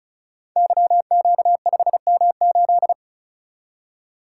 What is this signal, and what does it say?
Morse code, heard as a keyed tone, about 36 wpm, 700 Hz. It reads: YQ5M8
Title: Code Groups - 5 Letter:Numbers - 35wpm